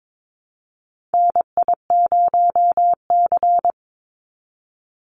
Morse code, heard as a keyed tone, about 22 wpm, 700 Hz. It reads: NI0C
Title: Call Signs Repeated 2x - 22wpm